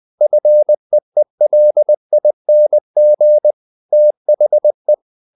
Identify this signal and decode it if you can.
Morse code, heard as a keyed tone, about 20 wpm, 600 Hz. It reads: FEELING THE
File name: the old man and the sea / the old man and the sea part 1 20wpm